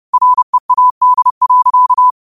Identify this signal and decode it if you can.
Morse code, heard as a keyed tone, about 30 wpm, 1000 Hz. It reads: READ.
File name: qotdfiles / 30wpm